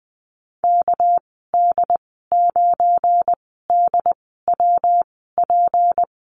Morse code, heard as a keyed tone, about 20 wpm, spaced as for 14 wpm, 700 Hz. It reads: KD9DWP
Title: 245 Lesson 39g - Letter z - Call Signs 20wpm